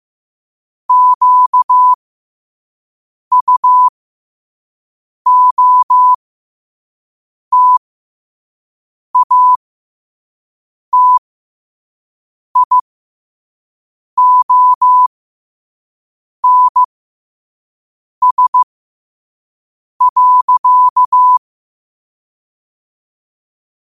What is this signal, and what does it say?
Morse code, heard as a keyed tone, about 15 wpm, spaced as for 5 wpm, 1000 Hz. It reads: QUOTATIONS.
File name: 5wpm